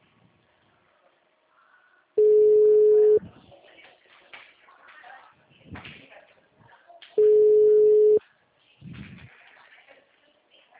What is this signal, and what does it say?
Ringback tone (Europe, 425 Hz).